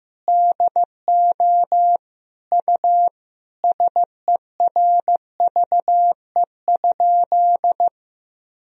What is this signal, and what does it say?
Morse code, heard as a keyed tone, about 15 wpm, 700 Hz. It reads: DO U SERVE?
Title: Sentences from Top 400 Words and Abbreviations 15wpm